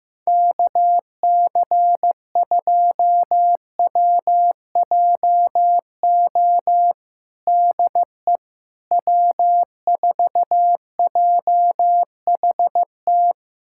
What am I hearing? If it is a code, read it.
Morse code, heard as a keyed tone, about 15 wpm, 700 Hz. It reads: KC2WJO DE W4JHT
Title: USA Callsign DE Callsign 15wpm